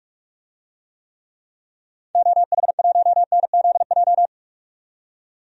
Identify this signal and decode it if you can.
Morse code, heard as a keyed tone, about 45 wpm, 700 Hz. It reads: OH1NZJ
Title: DXCC-100 Call Signs - 6 Characters - 45wpm